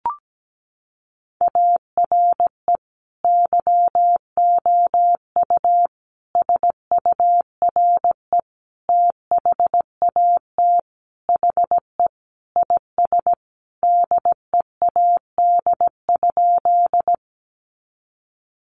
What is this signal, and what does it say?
Morse code, heard as a keyed tone, about 17 wpm, 700 Hz. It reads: ARE YOU SURE THAT HE IS DEAD?
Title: Sentences from Top 1500 Words 17wpm